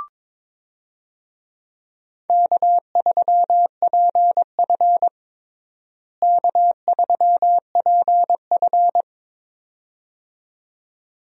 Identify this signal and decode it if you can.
Morse code, heard as a keyed tone, about 22 wpm, 700 Hz. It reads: K3PF K3PF
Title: USA Call Signs - 4 Characters - Repeated 2x - 22wpm